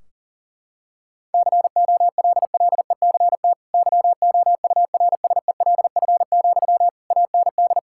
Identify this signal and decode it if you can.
Morse code, heard as a keyed tone, about 37 wpm, 700 Hz. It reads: COLLECT YOURSELF, AND
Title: Sentences from Top 1600 Words 40wpm